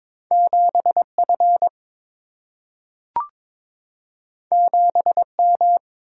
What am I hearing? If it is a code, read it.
Morse code, heard as a keyed tone, about 22 wpm, 700 Hz. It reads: 7F 7M
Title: Single-Letter Single-Number 22wpm